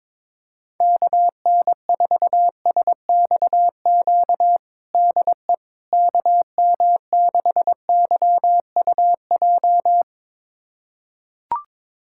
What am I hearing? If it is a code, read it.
Morse code, heard as a keyed tone, about 22 wpm, 700 Hz. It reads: KN4SXQ DE KM6YUJ
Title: USA Callsign DE Callsign 22wpm